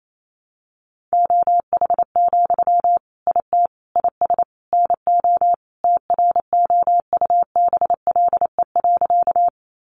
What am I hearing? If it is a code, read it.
Morse code, heard as a keyed tone, about 28 wpm, 700 Hz. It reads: OH, IT IS NO TROUBLE.
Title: Sentences from Top 1200 Words 28wpm